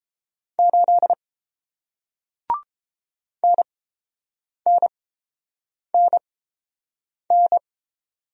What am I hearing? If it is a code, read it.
Morse code, heard as a keyed tone, about 30 wpm, spaced as for 7 wpm, 700 Hz. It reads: 8 NNNN